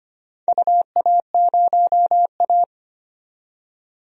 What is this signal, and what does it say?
Morse code, heard as a keyed tone, about 25 wpm, 700 Hz. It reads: UA0A
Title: DXCC-100 Call Signs - 4 Characters - 25wpm